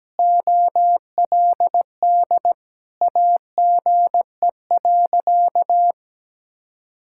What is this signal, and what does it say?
Morse code, heard as a keyed tone, about 17 wpm, 700 Hz. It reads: OLD AGE.
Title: Sentences from Top 1000 Words - Review Part 6 - 17wpm